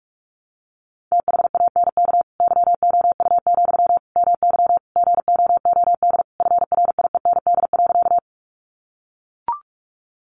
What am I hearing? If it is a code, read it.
Morse code, heard as a keyed tone, about 44 wpm, 700 Hz. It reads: THANK YOU, MY GOOD FRIEND.
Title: Sentences from Top 600 Words 45wpm